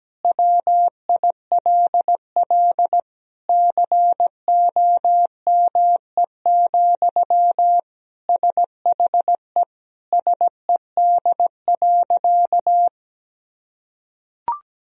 Morse code, heard as a keyed tone, about 17 wpm, 700 Hz. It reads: WILL COME, SHE SED.
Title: Sentences from Top 100 Words and Abbreviations 17wpm